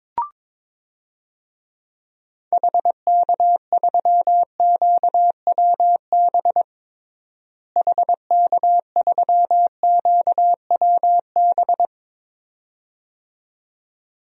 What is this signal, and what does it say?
Morse code, heard as a keyed tone, about 22 wpm, 700 Hz. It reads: HK3QWB HK3QWB
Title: DXCC-100 Call Signs - 6 Characters - Repeated 2x - 22wpm